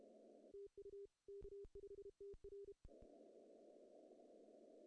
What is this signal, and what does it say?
Morse code, heard as a keyed tone, about 31 wpm, 410 Hz. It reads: TU K5TR
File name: CW / cqww-cw-k5tr